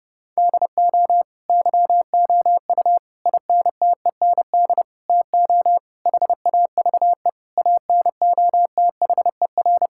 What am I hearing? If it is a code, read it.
Morse code, heard as a keyed tone, about 30 wpm, 700 Hz. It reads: DO YOU INTEND TO HAVE ANOTHER
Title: Sentences from Top 1600 Words 30wpm